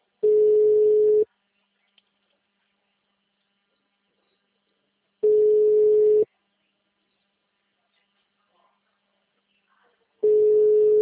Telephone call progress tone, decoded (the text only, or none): ringback tone